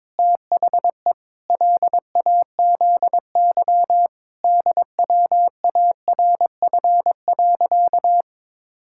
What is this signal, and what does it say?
Morse code, heard as a keyed tone, about 22 wpm, 700 Hz. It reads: THE LAZY DWARF.
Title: Pangram Challenge v2 - 22wpm